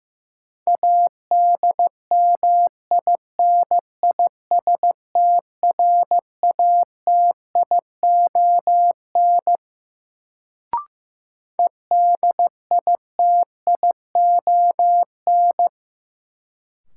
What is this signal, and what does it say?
Morse code, heard as a keyed tone, about 15 wpm, 700 Hz. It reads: ADMINISTRATION EDITION
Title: Sets of 5 Words - New Word Drill - 15wpm